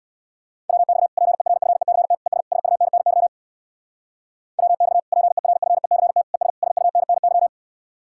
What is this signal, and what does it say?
Morse code, heard as a keyed tone, about 60 wpm, 700 Hz. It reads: WX PERFECT ES SUNNY WX PERFECT ES SUNNY
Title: Speed-Racing Common QSO Phrases - 50wpm